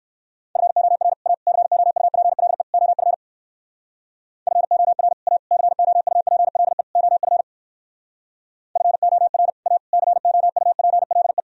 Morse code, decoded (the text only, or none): FOR A COUPLE OF FOR A COUPLE OF FOR A COUPLE